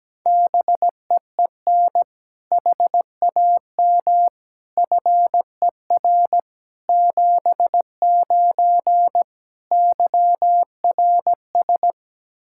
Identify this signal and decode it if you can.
Morse code, heard as a keyed tone, about 17 wpm, 700 Hz. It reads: BEEN HAM FER 79 YRS
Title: Practice QSO - Exchange 3 - More Info - 17wpm